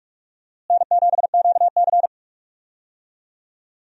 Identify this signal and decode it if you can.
Morse code, heard as a keyed tone, about 42 wpm, 700 Hz. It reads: N7QC
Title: USA Call Signs - 4 Characters - Repeated 2x - 45wpm